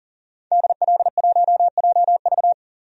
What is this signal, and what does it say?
Morse code, heard as a keyed tone, about 38 wpm, 700 Hz. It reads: DL1JV
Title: DXCC-100 Call Signs - 5 Characters - 40wpm